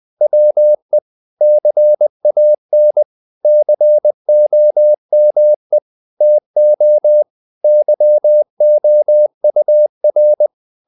Morse code, heard as a keyed tone, about 20 wpm, 600 Hz. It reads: WE CAN COME TO YOUR
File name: the old man and the sea / the old man and the sea part 1 20wpm